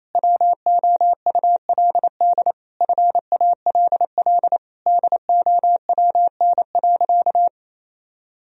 Morse code, heard as a keyed tone, about 28 wpm, 700 Hz. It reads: WOULD FALL DOWN.
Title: Sentences from Top 1000 Words - Review Part 2 - 28wpm